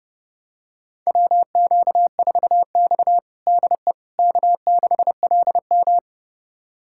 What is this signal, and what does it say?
Morse code, heard as a keyed tone, about 31 wpm, 700 Hz. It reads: WQ4X DE K6LM
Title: USA Callsign DE Callsign - 4 Characters - 30wpm